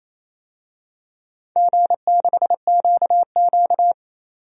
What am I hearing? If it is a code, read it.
Morse code, heard as a keyed tone, about 27 wpm, 690 Hz. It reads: G6QQ
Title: DXCC-100 Call Signs - 4 Characters - 28wpm